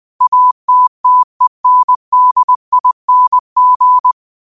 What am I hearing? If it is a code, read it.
Morse code, heard as a keyed tone, about 20 wpm, 1000 Hz. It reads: ATTENDING